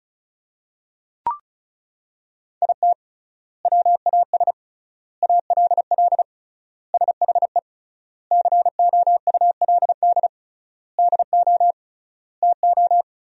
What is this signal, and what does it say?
Morse code, heard as a keyed tone, about 36 wpm, 700 Hz. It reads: IT WAS ALL SHE COULD DO TO
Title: 3x Word-Spacing - Sentences from Top 400 Words - 35wpm